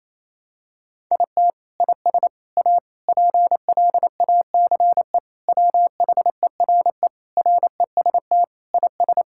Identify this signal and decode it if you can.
Morse code, heard as a keyed tone, about 28 wpm, 700 Hz. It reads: IT IS A PLACE WHERE REST IS
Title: Sentences from Top 1000 Words - Review Part 6 - 28wpm